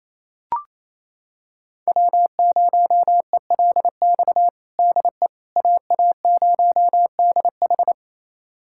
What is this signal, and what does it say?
Morse code, heard as a keyed tone, about 28 wpm, 700 Hz. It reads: W0ELX DE AA0DH
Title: USA Callsign DE Callsign - 5 Characters - 28wpm